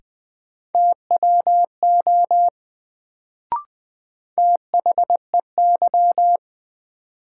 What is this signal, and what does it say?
Morse code, heard as a keyed tone, about 20 wpm, 700 Hz. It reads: TWO THEY
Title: Sets of 2 Words - New Word Drill - 20wpm